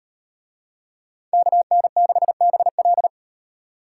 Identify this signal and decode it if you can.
Morse code, heard as a keyed tone, about 38 wpm, 700 Hz. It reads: KN6BL